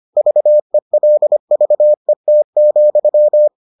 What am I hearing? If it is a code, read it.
Morse code, heard as a keyed tone, about 25 wpm, 600 Hz. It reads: VELVET,